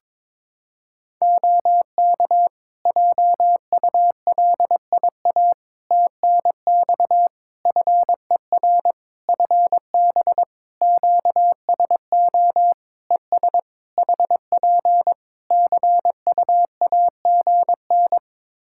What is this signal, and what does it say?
Morse code, heard as a keyed tone, about 22 wpm, 700 Hz. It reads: OK JULIA TNX FER FB QSO ES HP CUAGN
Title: Practice QSO - Exchange 4 - Ending - 22wpm